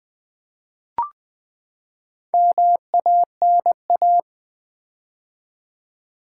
Morse code, heard as a keyed tone, about 20 wpm, 700 Hz. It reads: MANA
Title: Build A Word - 4 Letters At A Time - 10-Letter Words - 20wpm